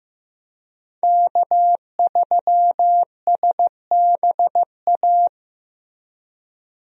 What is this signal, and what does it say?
Morse code, heard as a keyed tone, about 16 wpm, 700 Hz. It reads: K3SBA
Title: USA Call Signs - 5 Characters - Repeated 2x - 15wpm